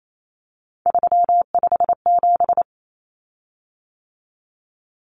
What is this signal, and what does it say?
Morse code, heard as a keyed tone, about 28 wpm, 700 Hz. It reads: 357
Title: Common Cut Numbers - 3-Digit 28wpm